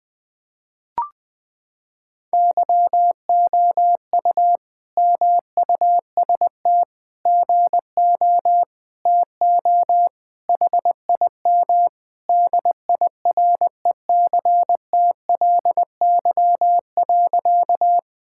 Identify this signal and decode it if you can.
Morse code, heard as a keyed tone, about 20 wpm, 700 Hz. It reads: YOU MUST GO TO HIM DIRECTLY.
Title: Sentences from Top 1000 Words - Review Part 5 - 20wpm